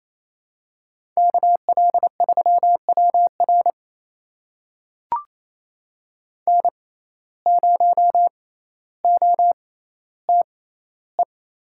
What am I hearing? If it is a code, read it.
Morse code, heard as a keyed tone, about 28 wpm, spaced as for 14 wpm, 700 Hz. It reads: KL3WR N0OTE
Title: USA Call Signs - 5 Characters - 6x Character Spacing - 28wpm